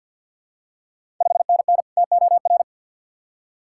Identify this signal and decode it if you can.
Morse code, heard as a keyed tone, about 53 wpm, 690 Hz. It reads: 5NN TOR
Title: Ontario QSO Party Exchange 50wpm